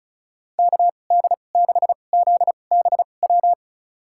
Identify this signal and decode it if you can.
Morse code, heard as a keyed tone, about 34 wpm, spaced as for 25 wpm, 700 Hz. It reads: KD6ZBW
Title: USA Call Signs - 6 Characters - 2x Character Spacing - 35wpm